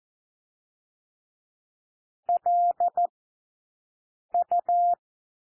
Morse code, heard as a keyed tone, about 14 wpm, 700 Hz. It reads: L U
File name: cwmp3 / W1 SOUND WORDS 14 wpm 0